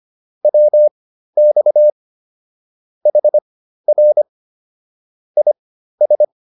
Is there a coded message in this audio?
Yes: Morse code from 0.5 to 6.2 s, heard as a keyed tone, about 25 wpm, spaced as for 13 wpm, 600 Hz. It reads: WX HR IS